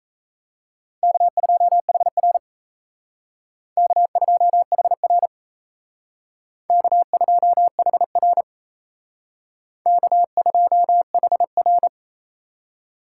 Morse code, heard as a keyed tone, about 36 wpm, 700 Hz. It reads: K2HR K2HR K2HR K2HR